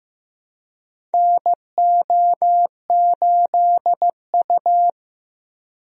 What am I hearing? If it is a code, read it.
Morse code, heard as a keyed tone, about 16 wpm, 700 Hz. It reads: NO8U
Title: USA Call Signs - 4 Characters - Repeated 3x - 15wpm